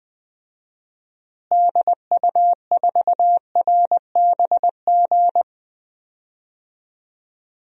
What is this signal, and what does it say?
Morse code, heard as a keyed tone, about 20 wpm, 700 Hz. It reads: DU4RBG
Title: DXCC-100 Call Signs - 6 Characters - Repeated 3x - 20wpm